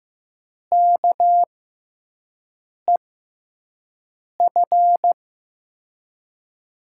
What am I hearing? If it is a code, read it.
Morse code, heard as a keyed tone, about 15 wpm, spaced as for 5 wpm, 700 Hz. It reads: KEF
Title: USA Call Signs - 6 Characters - 6x Character Spacing - 15wpm